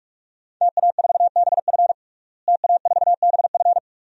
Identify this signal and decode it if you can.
Morse code, heard as a keyed tone, about 46 wpm, 700 Hz. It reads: TA4BF TA4BF
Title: DXCC-100 Call Signs - 5 Characters - Repeated 2x - 45wpm